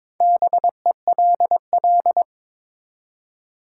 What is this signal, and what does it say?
Morse code, heard as a keyed tone, about 22 wpm, 700 Hz. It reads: BELL
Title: Four Letter Words - 22wpm